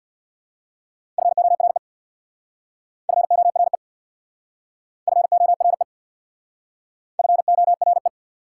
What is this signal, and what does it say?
Morse code, heard as a keyed tone, about 66 wpm, 690 Hz. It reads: FORE FORE FORE FORE